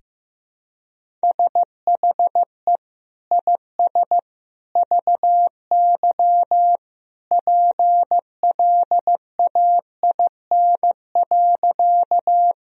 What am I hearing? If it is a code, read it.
Morse code, heard as a keyed tone, about 16 wpm, 700 Hz. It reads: SHE IS VY PLAIN.
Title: Sentences from Top 400 Words and Abbreviations 15wpm